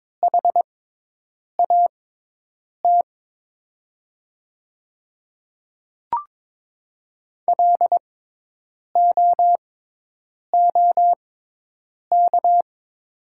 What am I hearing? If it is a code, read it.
Morse code, heard as a keyed tone, about 22 wpm, spaced as for 8 wpm, 700 Hz. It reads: HAT LOOK
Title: 6x Character Spacing - Top 100 Words - 22wpm